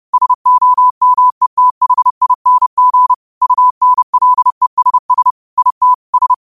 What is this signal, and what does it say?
Morse code, heard as a keyed tone, about 30 wpm, 1000 Hz. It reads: SOMETHING UNLESS IT S
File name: qotdfiles / 30wpm